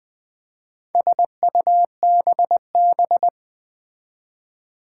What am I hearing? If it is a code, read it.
Morse code, heard as a keyed tone, about 20 wpm, 700 Hz. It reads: SUBB
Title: Four Challenging Letter-Number Combinations 20wpm